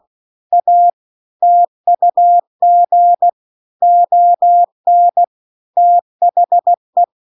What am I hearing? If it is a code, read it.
Morse code, heard as a keyed tone, about 16 wpm, 700 Hz. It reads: A TUG ON THE
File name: CW-The-Big-Twoheaded-River-2-16WPM